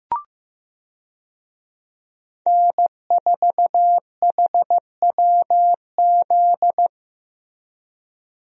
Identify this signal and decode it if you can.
Morse code, heard as a keyed tone, about 15 wpm, 700 Hz. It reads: N4HWZ
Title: USA Call Signs - 5 Characters - Repeated 2x - 15wpm